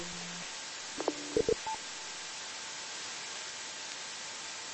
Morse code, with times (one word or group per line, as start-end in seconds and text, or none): none